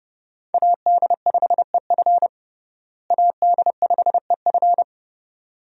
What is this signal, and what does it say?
Morse code, heard as a keyed tone, about 32 wpm, 700 Hz. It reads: AD5EF AD5EF
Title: USA Call Signs - 5 Characters - Repeated 2x - 30wpm